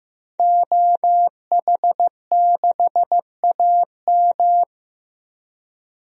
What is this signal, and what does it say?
Morse code, heard as a keyed tone, about 15 wpm, 700 Hz. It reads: OH6AM
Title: DXCC-100 Call Signs - 5 Characters - 15wpm